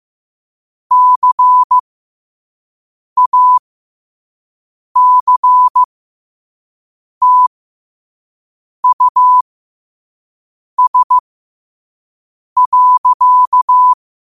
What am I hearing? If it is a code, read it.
Morse code, heard as a keyed tone, about 15 wpm, spaced as for 5 wpm, 1000 Hz. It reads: CACTUS.
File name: qotdfiles / 5wpm